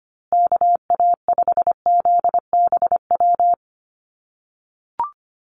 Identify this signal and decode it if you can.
Morse code, heard as a keyed tone, about 25 wpm, 700 Hz. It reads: KA5ZBW
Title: USA Call Signs Repeated 2x 25wpm